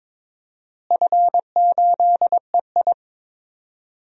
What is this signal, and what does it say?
Morse code, heard as a keyed tone, about 22 wpm, 700 Hz. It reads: F8EI
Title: DXCC-100 Call Signs - 4 Characters - Repeated 2x - 22wpm